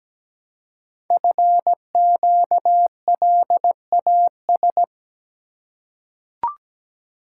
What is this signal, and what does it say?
Morse code, heard as a keyed tone, about 17 wpm, 700 Hz. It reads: FQLAS
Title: Code Groups - 5 Letters - 17wpm